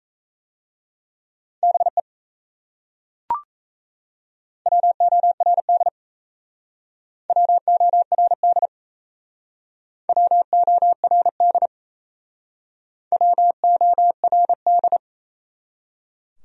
Morse code, heard as a keyed tone, about 37 wpm, 700 Hz. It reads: BE WORD WORD WORD WORD